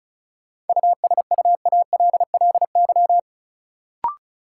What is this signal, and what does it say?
Morse code, heard as a keyed tone, about 36 wpm, 700 Hz. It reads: USUALLY
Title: Sentences from Top 1000 Words - Review Part 3 - New Word Drill - 35wpm